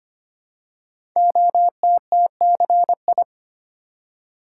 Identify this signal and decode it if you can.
Morse code, heard as a keyed tone, about 25 wpm, 700 Hz. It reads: OTTCI
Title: Code Groups - 5 Letters - 25wpm